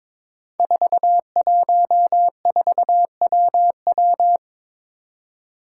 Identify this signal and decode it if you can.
Morse code, heard as a keyed tone, about 22 wpm, 700 Hz. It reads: 414WW
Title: Five Challenging Letter-Number Combinations 22wpm